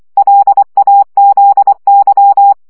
Morse code, heard as a keyed tone, about 24 wpm, 800 Hz. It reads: LAZY
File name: CW 20 baud